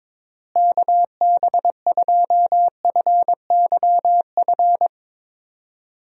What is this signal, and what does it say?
Morse code, heard as a keyed tone, about 22 wpm, 700 Hz. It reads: KB2FYF
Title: USA Call Signs - 6 Characters - Repeated 3x - 22wpm